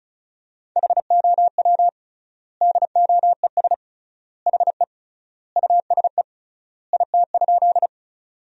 Morse code, heard as a keyed tone, about 36 wpm, 700 Hz. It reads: HOW DOES HE USE IT?
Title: 3x Word-Spacing - Sentences from Top 200 Words - 35wpm